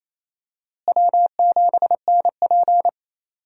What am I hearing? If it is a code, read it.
Morse code, heard as a keyed tone, about 28 wpm, 700 Hz. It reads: W7NP
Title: USA Call Signs - 4 Characters - Repeated 3x - 28wpm